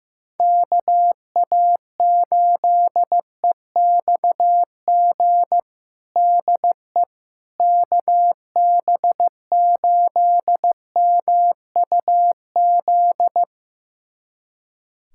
Morse code, heard as a keyed tone, about 15 wpm, 700 Hz. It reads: KA8EXG DE KB8MUZ
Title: USA Callsign DE Callsign 15wpm